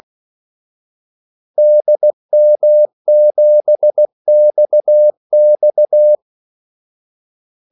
Morse code, heard as a keyed tone, about 16 wpm, 600 Hz. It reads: DM7XX
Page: Kaldesignaler 16 wpm